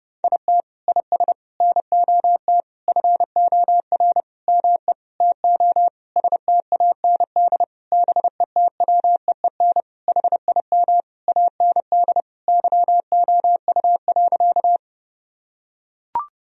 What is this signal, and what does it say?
Morse code, heard as a keyed tone, about 30 wpm, 700 Hz. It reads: IT IS NOT FOR ME TO STAND BETWEEN HIM AND YOU.
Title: Sentences from Top 1000 Words - Review Part 1 - 30wpm